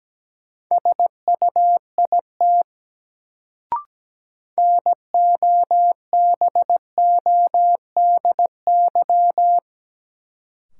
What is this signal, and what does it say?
Morse code, heard as a keyed tone, about 17 wpm, 700 Hz. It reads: SUIT NOBODY
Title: Sentences from Top 1600 Words - New Word Drill Alt - 17wpm